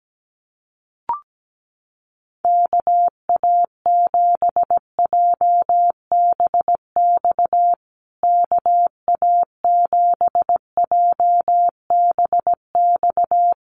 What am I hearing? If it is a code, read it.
Morse code, heard as a keyed tone, about 17 wpm, 700 Hz. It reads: KA7JBX KA7JBX
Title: USA Call Signs Repeated 3x 17wpm